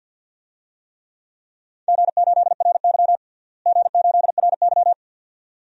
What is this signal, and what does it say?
Morse code, heard as a keyed tone, about 52 wpm, 700 Hz. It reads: G8RY G8RY
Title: DXCC-100 Call Signs - 4 Characters - Repeated 2x - 50wpm